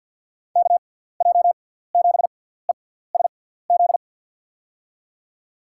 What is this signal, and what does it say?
Morse code, heard as a keyed tone, about 48 wpm, spaced as for 17 wpm, 700 Hz. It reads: KJ7ESZ
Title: USA Call Signs - 6 Characters - 6x Character Spacing - 50wpm